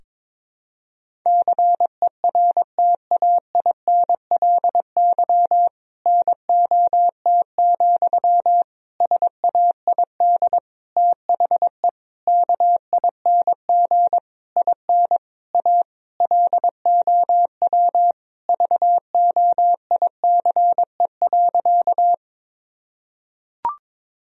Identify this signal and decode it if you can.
Morse code, heard as a keyed tone, about 22 wpm, 700 Hz. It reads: CERTAINLY NOT, SAID THE KING IN A LOW VOICE.
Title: Sentences from Top 1000 Words - Review Part 1 - 22wpm